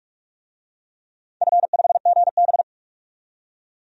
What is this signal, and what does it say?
Morse code, heard as a keyed tone, about 46 wpm, 700 Hz. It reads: F5GB
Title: DXCC-100 Call Signs - 4 Characters - 45wpm